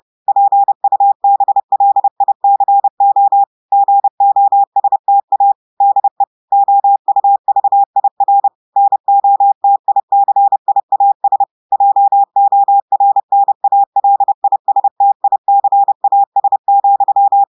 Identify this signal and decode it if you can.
Morse code, heard as a keyed tone, about 30 wpm, 800 Hz. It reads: PUBLICO GOSTA DE OUVIR NOTICIAS JORNALISTICAS,
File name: audios / portugais / portugais texte en clair 2 vitesse 30 mots minutes